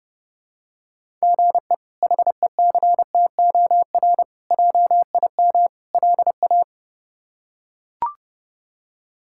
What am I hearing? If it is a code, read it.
Morse code, heard as a keyed tone, about 32 wpm, 700 Hz. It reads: GE HECTOR JIM LA
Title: K1USN SST QSO Practice - 30wpm